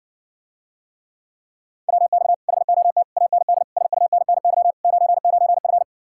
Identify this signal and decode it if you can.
Morse code, heard as a keyed tone, about 60 wpm, 700 Hz. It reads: WX HOT AND SUNNY 99F